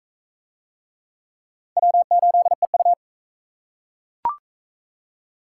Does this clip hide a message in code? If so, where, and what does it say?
Yes: Morse code from 1.8 to 2.9 s, heard as a keyed tone, about 42 wpm, 700 Hz. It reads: W8EU